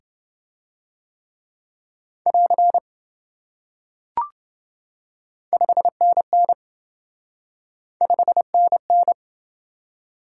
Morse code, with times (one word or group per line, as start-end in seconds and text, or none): 2.3-2.8 <AR>
5.5-6.5 5NN
8.0-9.1 5NN